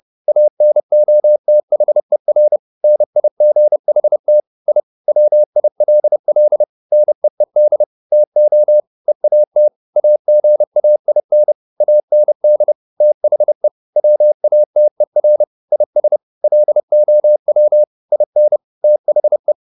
Morse code, heard as a keyed tone, about 30 wpm, 600 Hz. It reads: ANOTHER NIGHT I WILL NEED TO EAT AGAIN AND THE WATER IS LOW IN THE
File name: the old man and the sea / the old man and the sea 1 30wpm